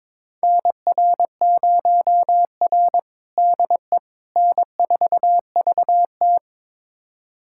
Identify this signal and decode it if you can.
Morse code, heard as a keyed tone, about 22 wpm, 700 Hz. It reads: NR0R DE N4VT
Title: USA Callsign DE Callsign - 4 Characters - 22wpm